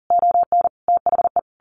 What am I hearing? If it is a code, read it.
Morse code, heard as a keyed tone, about 40 wpm, 700 Hz. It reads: ON THE
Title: A Sound Of Thunder 40wpm